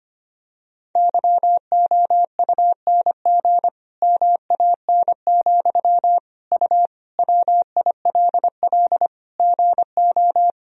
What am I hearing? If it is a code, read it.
Morse code, heard as a keyed tone, about 25 wpm, 700 Hz. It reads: YOUNG MAN, U WILL GO
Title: Sentences from Top 400 Words and Abbreviations 25wpm